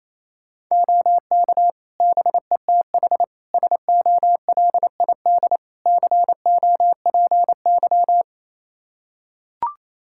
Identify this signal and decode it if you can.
Morse code, heard as a keyed tone, about 28 wpm, 700 Hz. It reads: OK BETH SOLID COPY
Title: Practice QSO - Exchange 3 - More Info - 28wpm